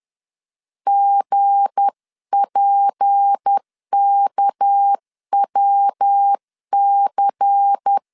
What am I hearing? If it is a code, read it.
Morse code, heard as a keyed tone, about 10 wpm, 800 Hz. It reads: GPKWC
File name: LEZIONI / 7 GF9Z / 7 GF 10wpm 2514